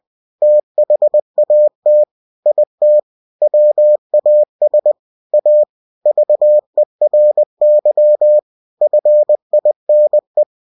Morse code, heard as a keyed tone, about 20 wpm, 600 Hz. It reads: THAT IT WAS A VERY FINE
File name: cw-books / winnie-the-pooh / winnie-20wpm-page-0041